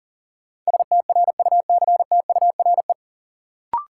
Morse code, heard as a keyed tone, about 40 wpm, 700 Hz. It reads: STRUCTURE
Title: Sentences from Top 1000 Words - Review Part 6 - New Word Drill - 40wpm